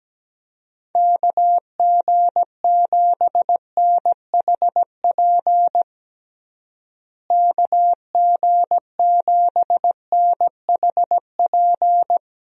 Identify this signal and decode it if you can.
Morse code, heard as a keyed tone, about 17 wpm, 700 Hz. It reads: KG7NHP KG7NHP
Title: USA Call Signs - 6 Characters - Repeated 3x - 17wpm